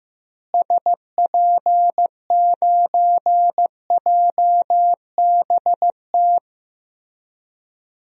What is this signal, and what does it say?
Morse code, heard as a keyed tone, about 16 wpm, 700 Hz. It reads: SP9JBT
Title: DXCC-100 Call Signs - 6 Characters - Repeated 3x - 15wpm